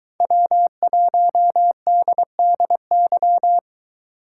Morse code, heard as a keyed tone, about 23 wpm, 700 Hz. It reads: W1DDY